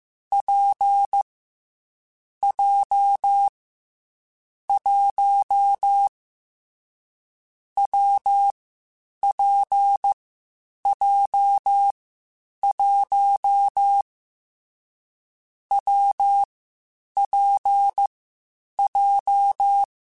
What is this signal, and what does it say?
Morse code, heard as a keyed tone, about 15 wpm, 800 Hz. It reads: P J 1 W P J 1 W P J
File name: Difficult Combos / Difficult Combos-mp3s / WPJ WPJ1 @13